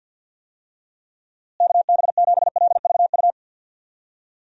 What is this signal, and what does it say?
Morse code, heard as a keyed tone, about 52 wpm, 700 Hz. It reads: KB7LVU
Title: USA Call Signs - 6 Characters - Repeated 3x - 50wpm